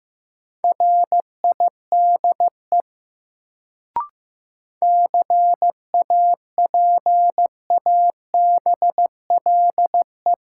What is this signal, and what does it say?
Morse code, heard as a keyed tone, about 15 wpm, 700 Hz. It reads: RIDE CAPABLE
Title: Sentences from Top 1200 Words - New Word Drill Alt - 15wpm